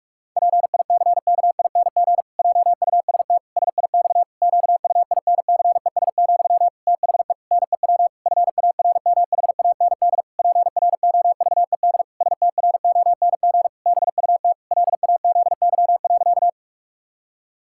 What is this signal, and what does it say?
Morse code, heard as a keyed tone, about 46 wpm, 700 Hz. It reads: PICKING JUST SIX QUINCES, THE NEW FARMHAND PROVED STRONG BUT LAZY.
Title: Pangram Challenge v2 - 45wpm